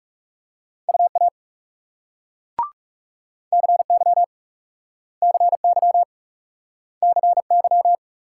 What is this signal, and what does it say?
Morse code, heard as a keyed tone, about 41 wpm, 700 Hz. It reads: UA CY CY CY